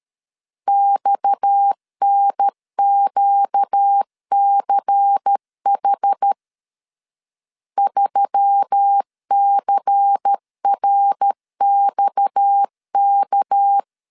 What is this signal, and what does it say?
Morse code, heard as a keyed tone, about 13 wpm, 800 Hz. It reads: XNQCH 3CRXK
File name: LEZIONI / 6 PKX0 / 6 PKX 12wpm 2024